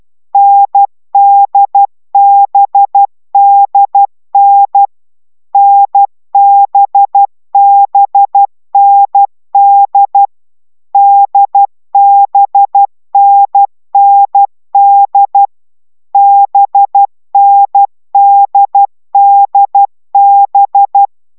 Morse code, heard as a keyed tone, about 12 wpm, 800 Hz. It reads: NDBDN NBBND DBNND BNDDB